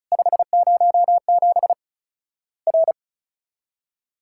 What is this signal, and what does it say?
Morse code, heard as a keyed tone, about 36 wpm, 700 Hz. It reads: 507 R
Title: Worked All Europe DX Contest - QTC Exchange - 35wpm